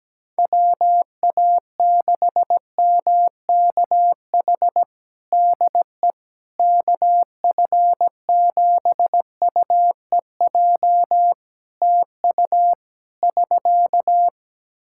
Morse code, heard as a keyed tone, about 17 wpm, 700 Hz. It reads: WA6MKH DE KF7UEJ TU <SK>
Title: K1USN SST QSO Practice 17wpm